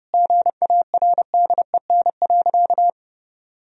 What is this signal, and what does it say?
Morse code, heard as a keyed tone, about 30 wpm, 700 Hz. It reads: GARDEN.
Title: Sentences from Top 1000 Words - Review Part 4 - 30wpm